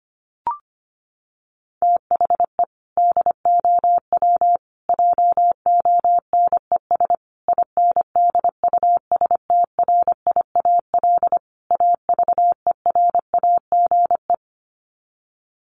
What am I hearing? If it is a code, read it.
Morse code, heard as a keyed tone, about 25 wpm, 700 Hz. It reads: THE DOW JONES INDUSTRIAL AVERAGE
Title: Sets of 5 Words 25wpm